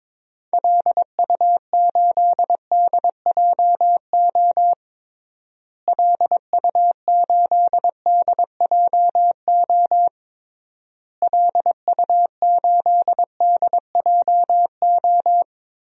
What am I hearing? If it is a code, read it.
Morse code, heard as a keyed tone, about 22 wpm, 700 Hz. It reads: LU8DJO LU8DJO LU8DJO
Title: DXCC-100 Call Signs - 6 Characters - Repeated 3x - 22wpm